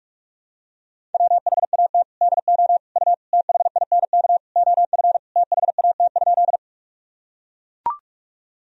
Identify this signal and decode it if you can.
Morse code, heard as a keyed tone, about 42 wpm, 700 Hz. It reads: WHAT DO U THINK OF THAT?
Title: Sentences from Top 200 Words and Abbreviations 45wpm